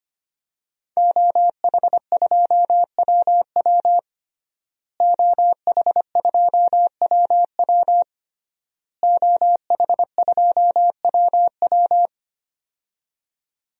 Morse code, heard as a keyed tone, about 25 wpm, 700 Hz. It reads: OH2WW OH2WW OH2WW
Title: DXCC-100 Call Signs - 5 Characters - Repeated 3x - 25wpm